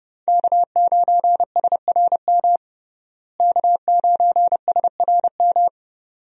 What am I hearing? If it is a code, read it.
Morse code, heard as a keyed tone, about 30 wpm, 700 Hz. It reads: K9SRM K9SRM
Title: USA Call Signs - 5 Characters - Repeated 2x - 30wpm